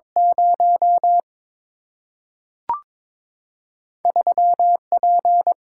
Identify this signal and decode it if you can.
Morse code, heard as a keyed tone, about 22 wpm, 700 Hz. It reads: 0 3 P
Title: Single-Letter Single-Number 22wpm